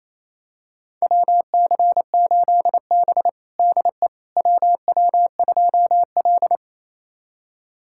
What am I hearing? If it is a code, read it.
Morse code, heard as a keyed tone, about 28 wpm, 700 Hz. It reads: WC8B DE WW2L
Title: USA Callsign DE Callsign - 4 Characters - 28wpm